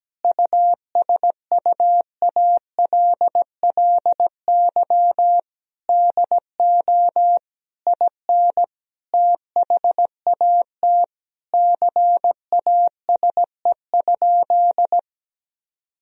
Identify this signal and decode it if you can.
Morse code, heard as a keyed tone, about 17 wpm, 700 Hz. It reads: USUALLY DO IN THAT CASE?
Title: Sentences from Top 1000 Words - Review Part 3 - 17wpm